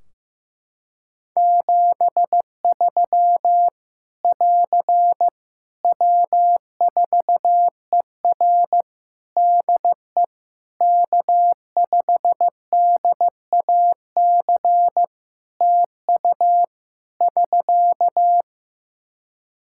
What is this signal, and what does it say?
Morse code, heard as a keyed tone, about 15 wpm, 700 Hz. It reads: 73 <AR> W4ER DE K5DAC TU <SK>
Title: K1USN SST QSO Practice 15wpm